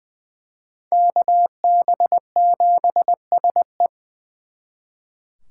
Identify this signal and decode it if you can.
Morse code, heard as a keyed tone, about 23 wpm, 700 Hz. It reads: KB7SE
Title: Speed Racing - USA 2x2 Call Signs 20wpm